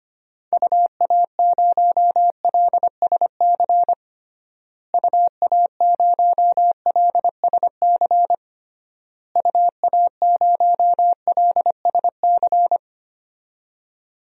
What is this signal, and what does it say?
Morse code, heard as a keyed tone, about 25 wpm, 700 Hz. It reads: UA0LSC UA0LSC UA0LSC
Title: DXCC-100 Call Signs - 6 Characters - Repeated 3x - 25wpm